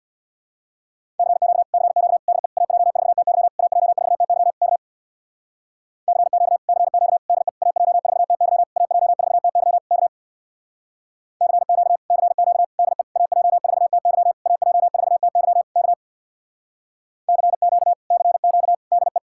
Morse code, heard as a keyed tone, about 62 wpm, 700 Hz. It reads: CQ CQ DE AJ4TY AJ4TY K CQ CQ DE AJ4TY AJ4TY K CQ CQ DE AJ4TY AJ4TY K CQ CQ DE
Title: Speed-Racing Common QSO Phrases - 50wpm